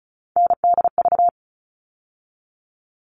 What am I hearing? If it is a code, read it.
Morse code, heard as a keyed tone, about 34 wpm, 700 Hz. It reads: NDV
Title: All Cut Numbers - 3-Digit 35wpm